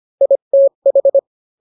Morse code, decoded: ITH